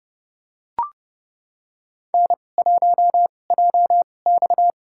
Morse code, heard as a keyed tone, about 30 wpm, spaced as for 22 wpm, 700 Hz. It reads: N1JX
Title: USA Call Signs - 4 Characters - 2x Character Spacing - 30wpm